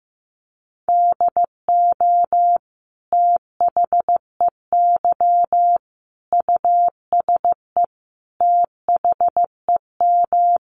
Morse code, heard as a keyed tone, about 14 wpm, 700 Hz. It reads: DO THEY USE THEM
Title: Sentences from Top 1300 Words 15wpm